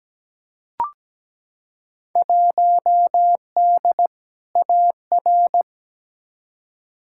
Morse code, heard as a keyed tone, about 17 wpm, 700 Hz. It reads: 1D AR
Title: Field Day Exchanges - 17wpm